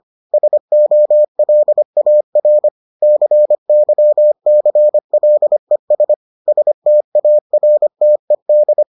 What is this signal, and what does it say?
Morse code, heard as a keyed tone, about 25 wpm, 600 Hz. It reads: SOLAR CYCLES STARTED